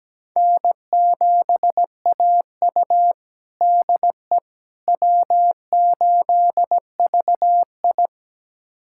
Morse code, heard as a keyed tone, about 17 wpm, 700 Hz. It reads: N7AU DE W8VI
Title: USA Callsign DE Callsign - 4 Characters - 17wpm